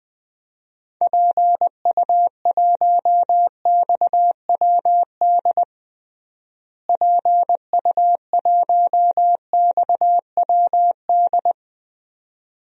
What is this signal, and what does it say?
Morse code, heard as a keyed tone, about 20 wpm, 700 Hz. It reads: PU1XWD PU1XWD
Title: DXCC-100 Call Signs - 6 Characters - Repeated 2x - 20wpm